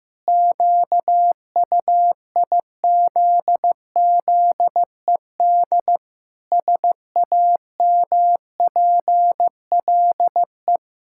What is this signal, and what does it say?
Morse code, heard as a keyed tone, about 15 wpm, 700 Hz. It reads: QUIZZED SAMPLE
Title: Pangram Challenge v2 - 15wpm